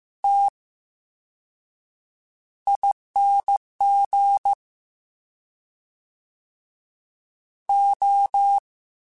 Morse code, heard as a keyed tone, about 15 wpm, 800 Hz. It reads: T I N G O
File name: Special INGs-mp3s / Special INGs-mp3s / VerySlowToFaster Ing @13